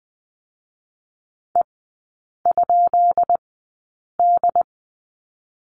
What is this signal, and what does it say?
Morse code, heard as a keyed tone, about 20 wpm, spaced as for 8 wpm, 700 Hz. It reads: E?D
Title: 218 Lesson 35e - Number 0 - Three Characters 20wpm